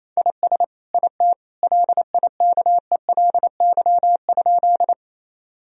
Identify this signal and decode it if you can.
Morse code, heard as a keyed tone, about 28 wpm, 700 Hz. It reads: IS IT LIKELY?
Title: Sentences from Top 1000 Words - Review Part 4 - 28wpm